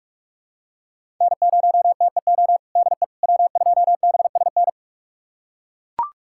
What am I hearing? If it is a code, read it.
Morse code, heard as a keyed tone, about 41 wpm, 700 Hz. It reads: N0TEO DE W2BSN
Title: USA Callsign DE Callsign - 5 Characters - 45wpm